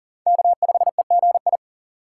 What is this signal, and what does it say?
Morse code, heard as a keyed tone, about 40 wpm, 700 Hz. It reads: K5EGI
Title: USA Call Signs - 5 Characters - 2x Character Spacing - 40wpm